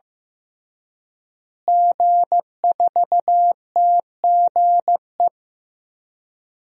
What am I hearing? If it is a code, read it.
Morse code, heard as a keyed tone, about 15 wpm, 700 Hz. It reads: G4TGE
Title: DXCC-100 Call Signs - 5 Characters - Repeated 3x - 15wpm